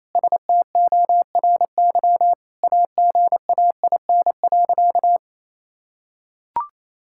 Morse code, heard as a keyed tone, about 27 wpm, 700 Hz. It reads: STORY AGAIN.
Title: Sentences from Top 1000 Words - Review Part 5 - 28wpm